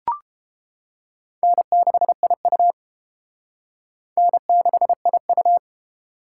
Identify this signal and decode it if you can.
Morse code, heard as a keyed tone, about 28 wpm, 700 Hz. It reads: N6IU N6IU